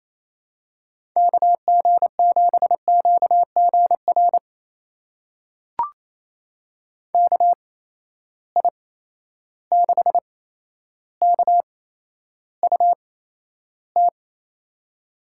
Morse code, heard as a keyed tone, about 28 wpm, spaced as for 12 wpm, 700 Hz. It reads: KG7QGR KI6KUT
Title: USA Call Signs - 6 Characters - 8x Character Spacing - 28wpm